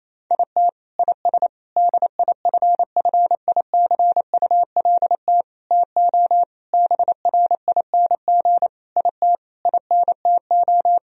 Morse code, heard as a keyed tone, about 28 wpm, 700 Hz. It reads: IT IS DIFFICULT TO BRING IT INTO
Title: Sentences from Top 1000 Words - Review Part 4 - 28wpm